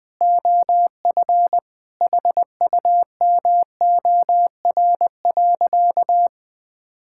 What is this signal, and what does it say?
Morse code, heard as a keyed tone, about 20 wpm, 700 Hz. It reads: OF HUMOR.
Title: Sentences with 2-Word Phrases - 20wpm